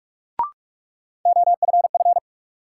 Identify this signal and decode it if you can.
Morse code, heard as a keyed tone, about 46 wpm, 700 Hz. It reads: OFF
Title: Sets of 2 Words - New Word Drill - 45wpm